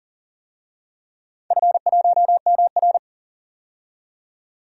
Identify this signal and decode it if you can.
Morse code, heard as a keyed tone, about 40 wpm, 700 Hz. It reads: F1MR
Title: DXCC-100 Call Signs - 4 Characters - 40wpm